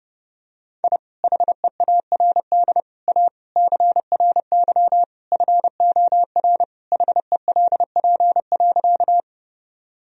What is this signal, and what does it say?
Morse code, heard as a keyed tone, about 30 wpm, 700 Hz. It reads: I HEARD A CRY FOR HELP.
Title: Sentences from Top 1000 Words - Review Part 2 - 30wpm